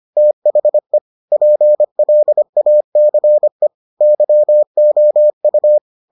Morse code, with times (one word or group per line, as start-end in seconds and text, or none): 0.2-1.0 THE
1.3-3.7 PLACE
4.0-5.8 YOU